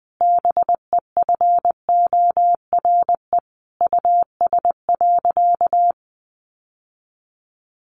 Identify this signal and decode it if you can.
Morse code, heard as a keyed tone, about 20 wpm, 700 Hz. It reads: BEFORE US.
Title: Sentences from Top 600 Words 20wpm